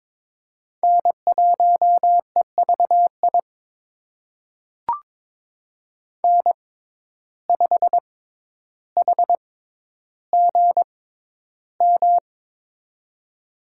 Morse code, heard as a keyed tone, about 22 wpm, spaced as for 11 wpm, 700 Hz. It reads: N1EVI N5HGM
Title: USA Call Signs - 6x Character Spacing - 22wpm